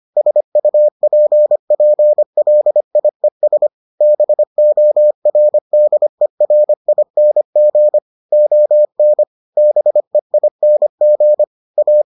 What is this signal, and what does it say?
Morse code, heard as a keyed tone, about 25 wpm, 600 Hz. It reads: SUPPLIES BORDERING ON BEING A